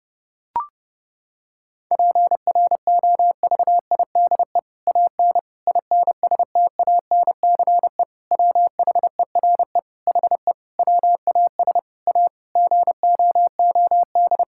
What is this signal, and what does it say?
Morse code, heard as a keyed tone, about 32 wpm, 700 Hz. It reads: PROVIDE AN INSTANCE WHERE HE WAS A GOOD
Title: Sentences from Top 1000 Words - Review Part 5 - 30wpm